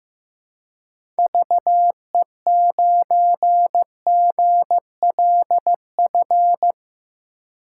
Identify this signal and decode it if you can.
Morse code, heard as a keyed tone, about 15 wpm, 700 Hz. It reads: VE9GLF
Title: K1USN SST QSO Practice - 15wpm